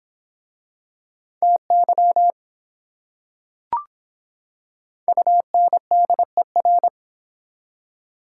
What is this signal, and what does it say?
Morse code, heard as a keyed tone, about 25 wpm, 700 Hz. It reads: TY UNDER